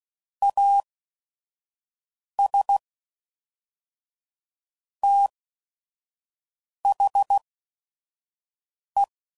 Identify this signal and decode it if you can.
Morse code, heard as a keyed tone, about 16 wpm, spaced as for 5 wpm, 800 Hz. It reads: AS THE
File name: CW Code Groups @5WPM / Technical Subjects / XL